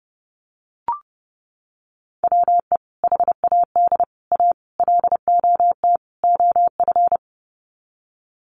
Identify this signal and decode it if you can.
Morse code, heard as a keyed tone, about 30 wpm, 700 Hz. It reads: WE HAD A LOT OF
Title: Sets of 5 Words Encore 30wpm